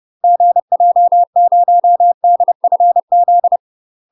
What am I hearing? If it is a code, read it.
Morse code, heard as a keyed tone, about 30 wpm, 700 Hz. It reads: GJ0DFZ
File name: UK MASTER QSOs TRAINING / 30WPM / UK MASTER QSOs 0029